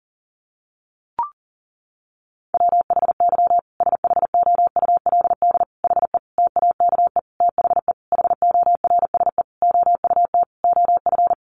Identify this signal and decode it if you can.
Morse code, heard as a keyed tone, about 40 wpm, 700 Hz. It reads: WHY SHOULD HE TAKE THE HORSE OUT OF
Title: Sentences from Top 1500 Words 40wpm